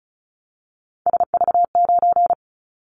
Morse code, heard as a keyed tone, about 36 wpm, 700 Hz. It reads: SV9
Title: Callsign Prefixes for Top 100 DXCC - 35wpm